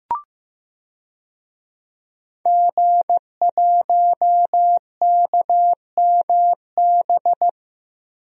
Morse code, heard as a keyed tone, about 15 wpm, 700 Hz. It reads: G1KMB
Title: DXCC-100 Call Signs - 5 Characters - Repeated 2x - 15wpm